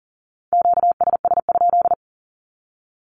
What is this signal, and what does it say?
Morse code, heard as a keyed tone, about 40 wpm, 700 Hz. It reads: QSS?
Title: Q-Codes 40wpm